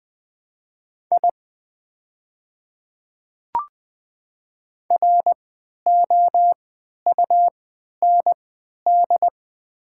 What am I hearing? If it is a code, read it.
Morse code, heard as a keyed tone, about 20 wpm, spaced as for 11 wpm, 700 Hz. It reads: I ROUND
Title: 3x Character Spacing - Top 200 Words - 20wpm